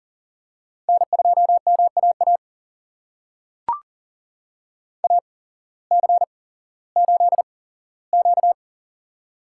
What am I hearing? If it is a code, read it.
Morse code, heard as a keyed tone, about 37 wpm, spaced as for 18 wpm, 700 Hz. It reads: N2MAA AC8Q
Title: USA Call Signs - 8x Character Spacing - 40wpm